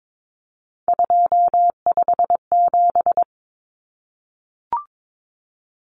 Morse code, heard as a keyed tone, about 22 wpm, 700 Hz. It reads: 257
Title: Common Cut Numbers - 3-Digit 22wpm